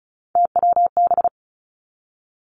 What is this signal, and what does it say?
Morse code, heard as a keyed tone, about 36 wpm, 700 Hz. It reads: TWB
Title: Three Challenging Letter-Number Combinations 35wpm